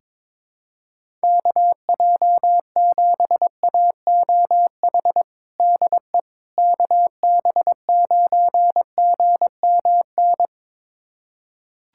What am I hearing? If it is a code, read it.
Morse code, heard as a keyed tone, about 22 wpm, 700 Hz. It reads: KJ7AOH DE KB9GMN
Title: USA Callsign DE Callsign - 6 Characters - 22wpm